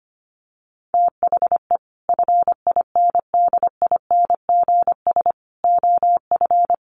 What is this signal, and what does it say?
Morse code, heard as a keyed tone, about 25 wpm, 700 Hz. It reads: THE FINDINGS OF
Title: Sets of 5 Words Encore2 25wpm